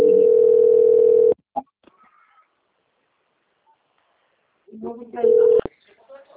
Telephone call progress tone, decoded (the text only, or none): ringback tone